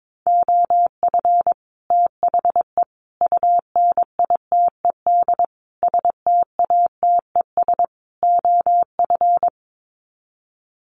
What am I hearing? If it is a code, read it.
Morse code, heard as a keyed tone, about 22 wpm, 700 Hz. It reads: OF THE UNITED STATES OF
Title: Sets of 5 Words Encore 22wpm